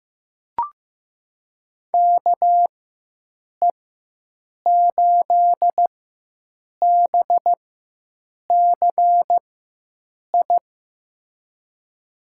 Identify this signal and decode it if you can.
Morse code, heard as a keyed tone, about 15 wpm, spaced as for 7 wpm, 700 Hz. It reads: KE8BCI
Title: USA Call Signs - 6 Characters - 4x Character Spacing - 15wpm